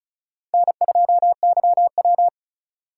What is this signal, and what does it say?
Morse code, heard as a keyed tone, about 36 wpm, 700 Hz. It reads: N2YW
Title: USA Call Signs - 4 Characters - 2x Character Spacing - 35wpm